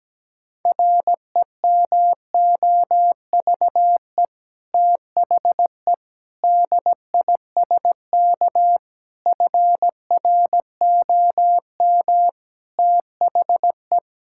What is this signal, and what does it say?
Morse code, heard as a keyed tone, about 17 wpm, 700 Hz. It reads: REMOVE THE DISK FROM THE
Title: Sentences from Top 1000 Words - Review Part 6 - 17wpm